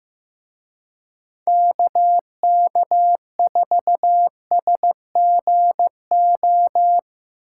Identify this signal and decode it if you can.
Morse code, heard as a keyed tone, about 17 wpm, 700 Hz. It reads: KK4SGO
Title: Speed Racing - USA 2x3 Call Signs 15wpm